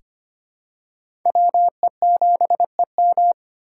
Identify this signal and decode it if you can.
Morse code, heard as a keyed tone, about 25 wpm, 700 Hz. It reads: WE7EM
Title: USA Call Signs - 4x Character Spacing - 25wpm